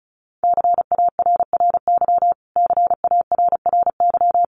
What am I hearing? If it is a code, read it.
Morse code, heard as a keyed tone, about 34 wpm, 700 Hz. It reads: CARRY CARRY
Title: Top 500 Words Repeated 35wpm